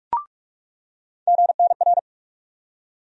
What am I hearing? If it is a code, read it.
Morse code, heard as a keyed tone, about 43 wpm, 700 Hz. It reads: GNR
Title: USA Call Sign Postfixes - 3 Letters 45wpm